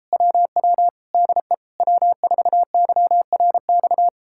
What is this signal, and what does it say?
Morse code, heard as a keyed tone, about 38 wpm, 700 Hz. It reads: WW DE W4YRX